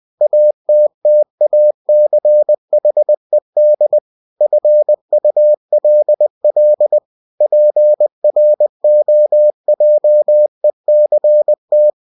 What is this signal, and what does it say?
Morse code, heard as a keyed tone, about 20 wpm, 600 Hz. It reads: ATTACHED FULL PROJECT
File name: cw-books / winnie-the-pooh / winnie-20wpm-page-0114